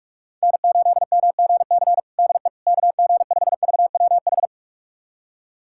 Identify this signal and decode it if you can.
Morse code, heard as a keyed tone, about 41 wpm, 700 Hz. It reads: N8MGC DE KG5VWH
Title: USA Callsign DE Callsign 45wpm